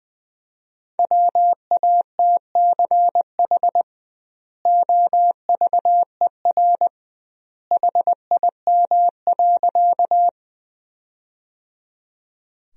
Morse code, heard as a keyed tone, about 20 wpm, 700 Hz. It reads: WATCH OVER HIM.
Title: 2x Word-Spacing - Sentences from Top 300 Words - 20wpm